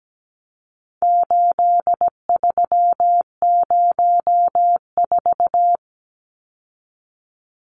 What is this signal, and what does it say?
Morse code, heard as a keyed tone, about 17 wpm, 700 Hz. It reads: 8304
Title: Numbers - 4-Digit 17wpm